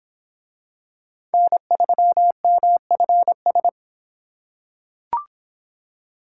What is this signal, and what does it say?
Morse code, heard as a keyed tone, about 24 wpm, 700 Hz. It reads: N3MFS